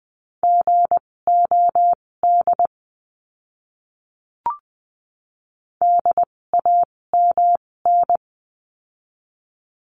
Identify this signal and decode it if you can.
Morse code, heard as a keyed tone, about 20 wpm, spaced as for 16 wpm, 700 Hz. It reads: GOD DAMN
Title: 256 Lesson 42a - Words 12-20wpm